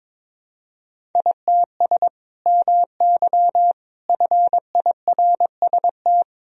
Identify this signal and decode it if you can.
Morse code, heard as a keyed tone, about 22 wpm, 700 Hz. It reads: ITS MY FIRST
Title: Sentences from Top 1000 Words - Review Part 1 - 22wpm